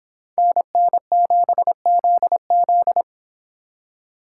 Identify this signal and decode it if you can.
Morse code, heard as a keyed tone, about 26 wpm, 700 Hz. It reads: NN7ZZ